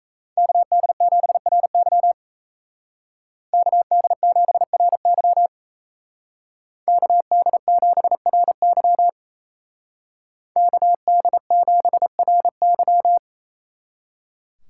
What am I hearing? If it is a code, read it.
Morse code, heard as a keyed tone, about 36 wpm, 700 Hz. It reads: KD7RY KD7RY KD7RY KD7RY